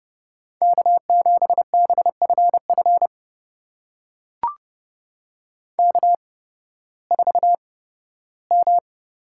Morse code, heard as a keyed tone, about 30 wpm, spaced as for 16 wpm, 700 Hz. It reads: K7BFF K4M
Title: USA Call Signs - 5 Characters - 8x Character Spacing - 30wpm